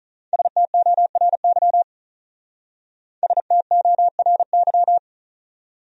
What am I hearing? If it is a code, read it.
Morse code, heard as a keyed tone, about 43 wpm, 700 Hz. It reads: STORY STORY
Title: Speed-Racing Top 300 Words 35wpm